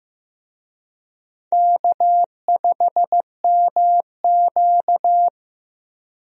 Morse code, heard as a keyed tone, about 15 wpm, 700 Hz. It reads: K5MQ
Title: USA Call Signs - 4 Characters - 15wpm